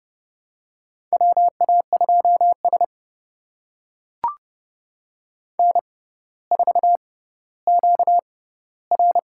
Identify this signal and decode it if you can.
Morse code, heard as a keyed tone, about 30 wpm, spaced as for 15 wpm, 700 Hz. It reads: WA2S N4QR
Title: USA Call Signs - 4 Characters - 6x Character Spacing - 30wpm